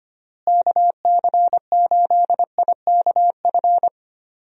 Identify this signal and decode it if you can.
Morse code, heard as a keyed tone, about 25 wpm, 700 Hz. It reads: KC8IKF
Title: USA Call Signs - 6 Characters - Repeated 3x - 25wpm